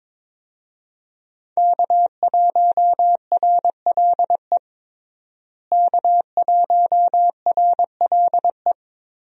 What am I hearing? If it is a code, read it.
Morse code, heard as a keyed tone, about 22 wpm, 700 Hz. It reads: K1RLE K1RLE
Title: USA Call Signs - 5 Characters - Repeated 3x - 22wpm